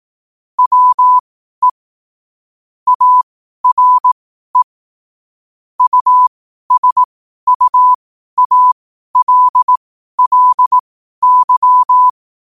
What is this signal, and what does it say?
Morse code, heard as a keyed tone, about 18 wpm, spaced as for 12 wpm, 1000 Hz. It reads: WE ARE USUALLY
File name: qotdfiles / 12wpm